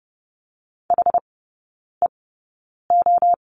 Morse code, heard as a keyed tone, about 30 wpm, spaced as for 9 wpm, 700 Hz. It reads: HEO
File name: Lesson 19e - Letter w - Three Characters 30wpm